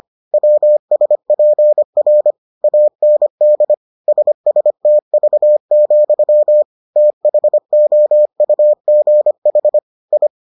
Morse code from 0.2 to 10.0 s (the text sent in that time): WSPR AND SSTV, THOUGH